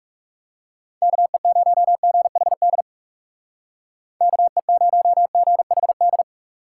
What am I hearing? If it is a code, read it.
Morse code, heard as a keyed tone, about 36 wpm, 700 Hz. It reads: KE0GHD KE0GHD
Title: Speed Racing - USA 2x3 Call Signs 30wpm